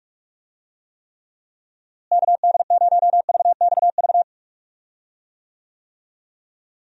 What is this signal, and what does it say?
Morse code, heard as a keyed tone, about 48 wpm, 700 Hz. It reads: KD0VXV
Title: USA Call Signs - 6 Characters - Repeated 3x - 45wpm